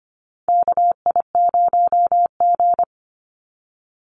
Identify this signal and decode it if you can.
Morse code, heard as a keyed tone, about 25 wpm, 700 Hz. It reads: KI0G
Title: NA SOTA Chaser Callsigns 25wpm